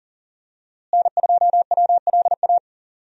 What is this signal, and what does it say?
Morse code, heard as a keyed tone, about 40 wpm, 700 Hz. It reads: N2WLA
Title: USA Call Signs - 5 Characters - Repeated 2x - 40wpm